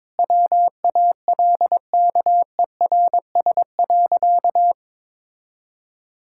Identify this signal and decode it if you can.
Morse code, heard as a keyed tone, about 21 wpm, 700 Hz. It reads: WALKERS.
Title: Pangram Challenge v2 - 22wpm